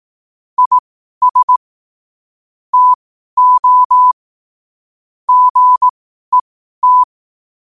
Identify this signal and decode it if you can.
Morse code, heard as a keyed tone, about 18 wpm, spaced as for 12 wpm, 1000 Hz. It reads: IS TO GET
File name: qotdfiles / 12wpm